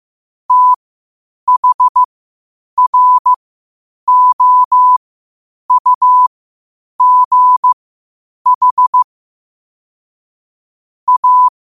Morse code, heard as a keyed tone, about 15 wpm, spaced as for 8 wpm, 1000 Hz. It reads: THROUGH A
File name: qotdfiles / 8wpm